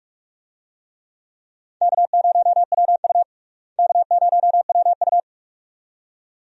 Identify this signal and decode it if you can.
Morse code, heard as a keyed tone, about 42 wpm, 700 Hz. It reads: K0WU K0WU
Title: USA Call Signs - 4 Characters - Repeated 2x - 45wpm